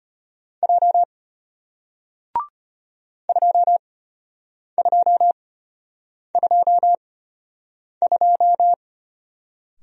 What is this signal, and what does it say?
Morse code, heard as a keyed tone, about 33 wpm, spaced as for 8 wpm, 690 Hz. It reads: J 2222